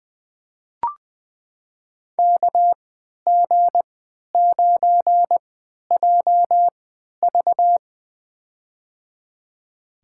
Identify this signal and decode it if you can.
Morse code, heard as a keyed tone, about 20 wpm, spaced as for 11 wpm, 700 Hz. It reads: KG9JV
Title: USA Call Signs - 3x Character Spacing - 20wpm